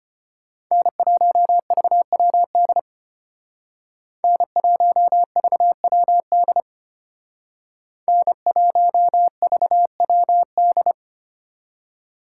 Morse code, heard as a keyed tone, about 33 wpm, 700 Hz. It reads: N1VWD N1VWD N1VWD